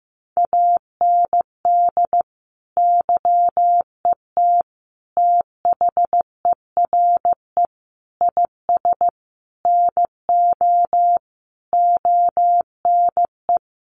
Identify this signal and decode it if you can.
Morse code, heard as a keyed tone, about 15 wpm, 700 Hz. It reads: AND YET THERE IS NO ONE
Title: Sentences from Top 1000 Words 15wpm